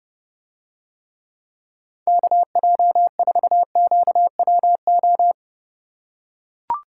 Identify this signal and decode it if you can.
Morse code, heard as a keyed tone, about 29 wpm, 700 Hz. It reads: KJ4QWO